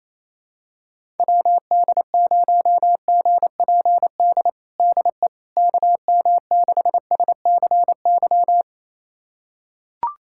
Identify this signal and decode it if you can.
Morse code, heard as a keyed tone, about 27 wpm, 700 Hz. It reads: WD0GPD DE KM6SCY
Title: USA Callsign DE Callsign 28wpm